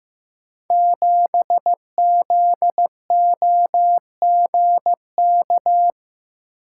Morse code, heard as a keyed tone, about 15 wpm, 700 Hz. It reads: 7ZOGK
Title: Code Groups - 5 Letter:Numbers - 15wpm